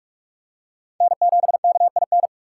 Morse code, heard as a keyed tone, about 42 wpm, 700 Hz. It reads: N7KIN